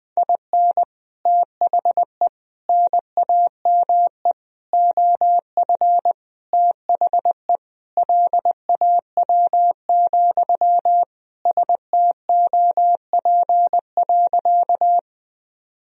Morse code, heard as a keyed tone, about 20 wpm, 700 Hz. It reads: IN THE NAME OF THE LAW, STOP.
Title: Sentences from Top 1000 Words - Review Part 3 - 20wpm